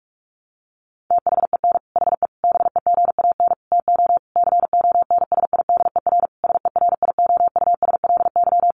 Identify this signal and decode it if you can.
Morse code, heard as a keyed tone, about 44 wpm, 700 Hz. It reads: THEN HE BEGAN TO CONSIDER SERIOUSLY
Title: Sentences from Top 1200 Words 45wpm